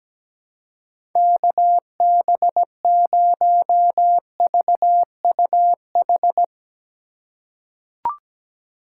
Morse code, heard as a keyed tone, about 17 wpm, 700 Hz. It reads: KB0VUH
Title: USA Call Signs - 6 Characters - 6x Character Spacing - 17wpm